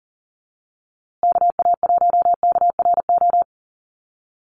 Morse code, heard as a keyed tone, about 40 wpm, 700 Hz. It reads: KA1KRO
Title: USA Call Signs Repeated 2x 40wpm